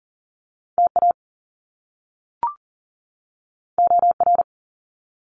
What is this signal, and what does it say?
Morse code, heard as a keyed tone, about 41 wpm, 700 Hz. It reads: TA OR
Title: Top Two Letter Combinations in Words 40wpm